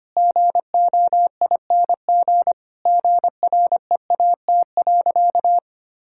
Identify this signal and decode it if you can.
Morse code, heard as a keyed tone, about 25 wpm, 700 Hz. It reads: GOING GREAT.
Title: Sentences from Top 1000 Words - Review Part 5 - 25wpm